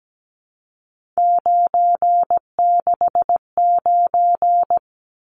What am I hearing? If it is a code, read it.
Morse code, heard as a keyed tone, about 17 wpm, 700 Hz. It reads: 969
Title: Numbers - 3-Digit 17wpm